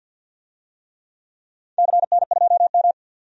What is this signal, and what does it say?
Morse code, heard as a keyed tone, about 46 wpm, 700 Hz. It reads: CN2M
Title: DXCC-100 Call Signs - 4 Characters - 50wpm